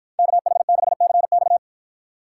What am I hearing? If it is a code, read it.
Morse code, heard as a keyed tone, about 50 wpm, 700 Hz. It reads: KH6CX